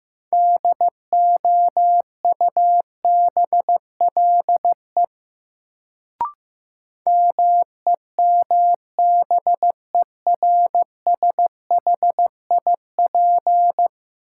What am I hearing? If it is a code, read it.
Morse code, heard as a keyed tone, about 15 wpm, 700 Hz. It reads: DOUBLE MEMBERSHIP
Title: Sentences from Top 1400 Words - New Word Drill - 15wpm